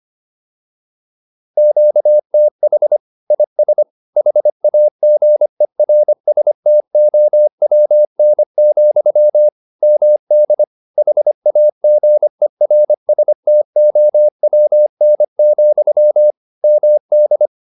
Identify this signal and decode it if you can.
Morse code, heard as a keyed tone, about 25 wpm, 600 Hz. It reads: QTH IS HAGERSTOWN, MD HAGERSTOWN, MD
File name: QSO307a 25